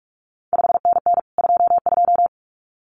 Morse code, heard as a keyed tone, about 46 wpm, 700 Hz. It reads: 5NN 22
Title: CQ WW DX Contest Exchange Encore - 45wpm